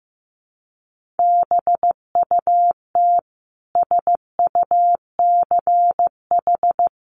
Morse code, heard as a keyed tone, about 15 wpm, 700 Hz. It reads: BUT SUCH
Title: Sentences from Top 1500 Words 15wpm